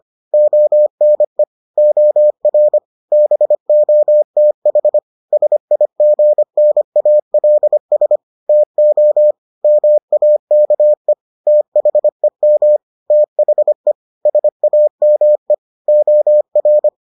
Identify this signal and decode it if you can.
Morse code, heard as a keyed tone, about 25 wpm, 600 Hz. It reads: ONE OR BOTH SIGNALS TO MAKE THEM THE SAME OR